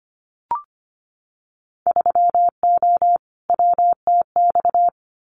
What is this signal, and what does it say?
Morse code, heard as a keyed tone, about 25 wpm, 700 Hz. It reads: 3O WTX
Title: Winter Field Day Exchanges Encore 25wpm